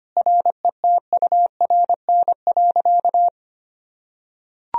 Morse code, heard as a keyed tone, about 25 wpm, 700 Hz. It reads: RETURN.
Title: Sentences from Top 1000 Words - Review Part 1 - 25wpm